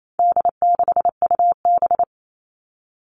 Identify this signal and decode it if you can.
Morse code, heard as a keyed tone, about 28 wpm, 700 Hz. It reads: D6UB
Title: All Cut Numbers - 4-Digit 28wpm